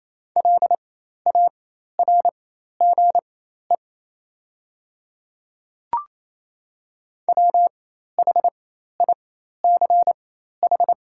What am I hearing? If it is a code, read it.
Morse code, heard as a keyed tone, about 28 wpm, spaced as for 13 wpm, 700 Hz. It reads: LARGE WHICH
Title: 4x Character Spacing - Top 200 Words - 28wpm